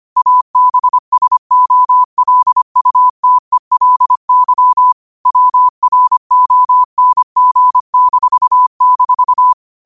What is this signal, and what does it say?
Morse code, heard as a keyed tone, about 25 wpm, 1000 Hz. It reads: ABSOLUTELY WRONG--
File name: qotdfiles / 25wpm